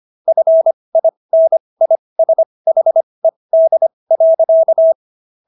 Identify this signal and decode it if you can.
Morse code, heard as a keyed tone, about 25 wpm, spaced as for 20 wpm, 650 Hz. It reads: FINISHED.